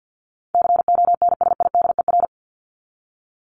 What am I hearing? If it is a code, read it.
Morse code, heard as a keyed tone, about 50 wpm, 700 Hz. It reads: CONSIDER
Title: ING Suffix - Top 100 Words with Base Words 50wpm